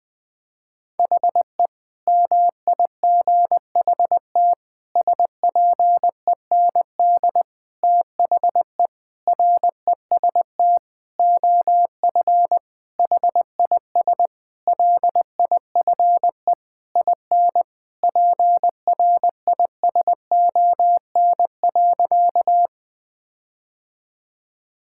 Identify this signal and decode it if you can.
Morse code, heard as a keyed tone, about 20 wpm, 700 Hz. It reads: HE MIGHT SPEND THE REST OF HIS LIFE IN PRISON.
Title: Sentences with 8-Word Phrases - Spoken Beginning - 20wpm